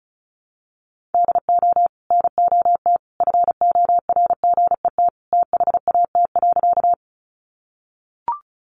Morse code, heard as a keyed tone, about 34 wpm, 700 Hz. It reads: DO NOT FORGET THAT.
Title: Sentences from Top 1200 Words 35wpm